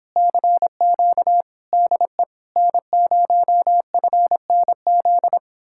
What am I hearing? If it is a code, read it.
Morse code, heard as a keyed tone, about 31 wpm, 700 Hz. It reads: CQ DE N0FNZ